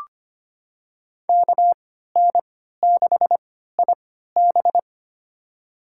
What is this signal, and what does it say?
Morse code, heard as a keyed tone, about 25 wpm, spaced as for 14 wpm, 700 Hz. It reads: KN6IB
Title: USA Call Signs - 5 Characters - 3x Character Spacing - 25wpm